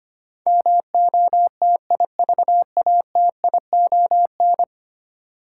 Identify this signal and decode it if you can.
Morse code, heard as a keyed tone, about 25 wpm, 700 Hz. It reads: MOTIVATION
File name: 10 Letter Words - First 3 Letters Given - 25wpm